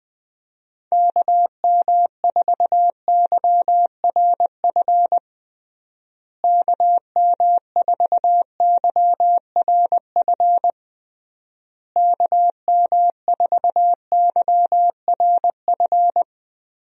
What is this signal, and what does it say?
Morse code, heard as a keyed tone, about 20 wpm, 700 Hz. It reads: KM4YRF KM4YRF KM4YRF
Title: USA Call Signs - 6 Characters - Repeated 3x - 20wpm